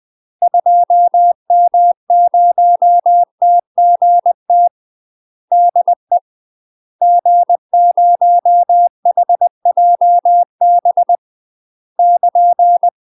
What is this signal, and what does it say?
Morse code, heard as a keyed tone, about 20 wpm, 700 Hz. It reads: 2M0TGT DE G0HJB <KN>
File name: UK MASTER QSOs TRAINING/20WPM